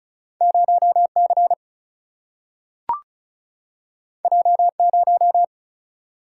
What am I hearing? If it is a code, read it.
Morse code, heard as a keyed tone, about 36 wpm, 700 Hz. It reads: 0 C J 0
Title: Single-Letter Single-Number 35wpm